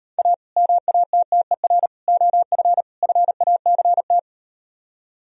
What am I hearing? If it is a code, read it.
Morse code, heard as a keyed tone, about 38 wpm, 700 Hz. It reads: A MATTER OF FACT